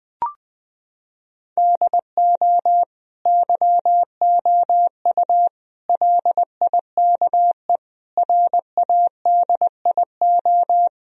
Morse code, heard as a keyed tone, about 19 wpm, 700 Hz. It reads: DO YOU LIKE RADIO
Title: Sentences from Top 1000 Words - Review Part 4 - 20wpm